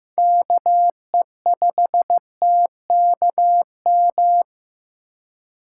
Morse code, heard as a keyed tone, about 15 wpm, 700 Hz. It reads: KE5TKM
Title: USA Call Signs - 6 Characters - 3x Character Spacing - 15wpm